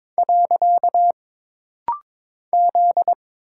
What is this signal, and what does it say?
Morse code, heard as a keyed tone, about 22 wpm, 700 Hz. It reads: . Z
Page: This rapid-fire timing is geared to push your ICR (Instant Character Recognition) skills.